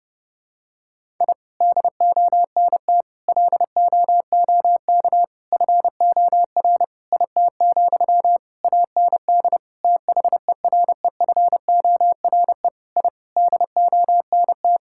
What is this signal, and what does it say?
Morse code, heard as a keyed tone, about 32 wpm, 700 Hz. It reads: I DONT LOOK FOR IT, AND THEREFORE I DONT
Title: Sentences from Top 1000 Words - Review Part 5 - 30wpm